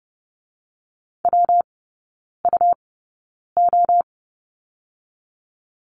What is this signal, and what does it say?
Morse code, heard as a keyed tone, about 30 wpm, 700 Hz. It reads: W U O
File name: Lesson 19e - Letter w - Three Characters 30wpm